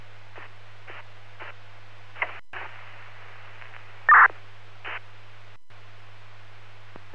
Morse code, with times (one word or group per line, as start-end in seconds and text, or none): none